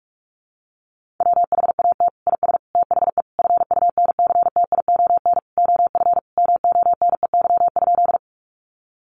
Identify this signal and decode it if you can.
Morse code, heard as a keyed tone, about 44 wpm, 700 Hz. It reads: WHAT IS THE FUNCTION OF MONEY?
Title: Sentences from Top 1100 Words 45wpm